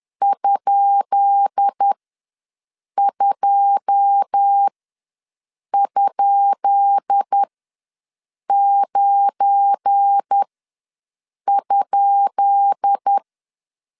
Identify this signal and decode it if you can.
Morse code, heard as a keyed tone, about 10 wpm, spaced as for 6 wpm, 800 Hz. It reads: ?2?9?
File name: LEZIONI / 10 SIGNS / 10B INT ws 10wpm 4134